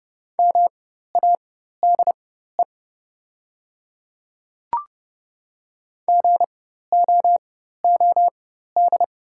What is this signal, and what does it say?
Morse code, heard as a keyed tone, about 30 wpm, spaced as for 14 wpm, 700 Hz. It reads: MADE GOOD
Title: 4x Character Spacing - Top 200 Words - 30wpm